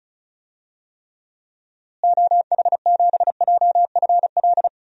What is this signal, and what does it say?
Morse code, heard as a keyed tone, about 34 wpm, 700 Hz. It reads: OH7JFL
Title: DXCC-100 Call Signs - 6 Characters - Repeated 3x - 35wpm